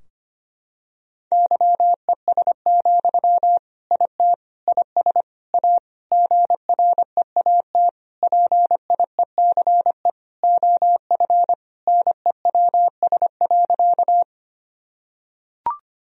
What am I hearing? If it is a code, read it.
Morse code, heard as a keyed tone, about 25 wpm, 700 Hz. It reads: YES, IT IS A GREAT PIECE OF NEWS.
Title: Sentences from Top 1000 Words - Review Part 2 - 25wpm